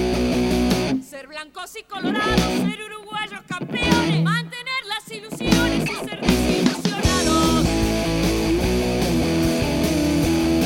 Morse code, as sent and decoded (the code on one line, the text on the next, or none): none